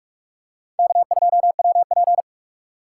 Morse code, heard as a keyed tone, about 44 wpm, 690 Hz. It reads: K2WP
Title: USA Call Signs - 4 Characters - 3x Character Spacing - 45wpm